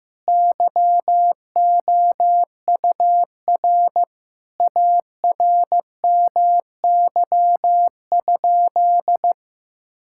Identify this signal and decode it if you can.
Morse code, heard as a keyed tone, about 15 wpm, 700 Hz. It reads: YOUR ARMY?
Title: Sentences from Top 1000 Words - Review Part 5 - 15wpm